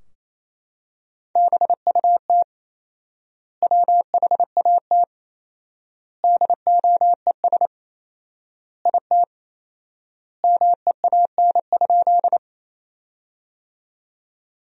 Morse code, heard as a keyed tone, about 28 wpm, 700 Hz. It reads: BUT WHAT DOES IT MEAN?
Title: 4x Word-Spacing - Sentences from Top 200 Words - 28wpm